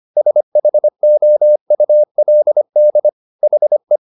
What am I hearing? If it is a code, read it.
Morse code, heard as a keyed tone, about 25 wpm, 600 Hz. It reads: SHOULD HE